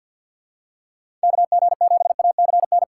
Morse code, heard as a keyed tone, about 48 wpm, 700 Hz. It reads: KG7ACN
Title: USA Call Signs - 6 Characters - 6x Character Spacing - 50wpm